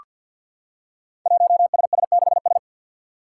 Morse code, heard as a keyed tone, about 47 wpm, 700 Hz. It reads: 1SS6S
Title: Five Challenging Letter-Number Combinations 50wpm